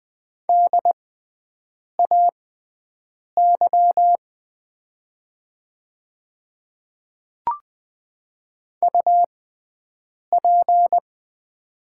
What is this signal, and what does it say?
Morse code, heard as a keyed tone, about 20 wpm, spaced as for 7 wpm, 700 Hz. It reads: DAY UP
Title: 6x Character Spacing - Top 100 Words - 20wpm